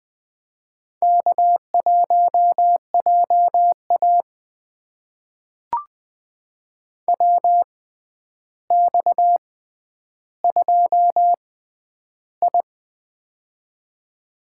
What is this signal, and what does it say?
Morse code, heard as a keyed tone, about 20 wpm, spaced as for 10 wpm, 700 Hz. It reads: K1JA WX2I
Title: USA Call Signs - 4 Characters - 6x Character Spacing - 20wpm